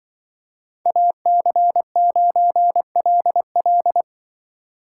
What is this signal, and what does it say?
Morse code, heard as a keyed tone, about 24 wpm, 700 Hz. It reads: AC9LL
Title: Speed Racing - USA 2x2 Call Signs 20wpm